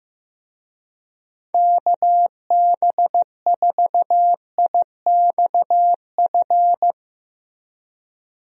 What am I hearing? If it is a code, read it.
Morse code, heard as a keyed tone, about 15 wpm, 700 Hz. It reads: KB4IXF
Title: USA Call Signs - 6 Characters - Repeated 3x - 15wpm